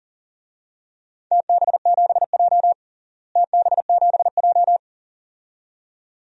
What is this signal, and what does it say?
Morse code, heard as a keyed tone, about 37 wpm, 700 Hz. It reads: TB7J TB7J
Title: DXCC-100 Call Signs - 4 Characters - Repeated 2x - 40wpm